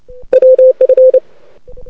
Morse code, heard as a keyed tone, about 29 wpm, 500 Hz. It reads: WF
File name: rec contest / arrl cw 2018 / test 32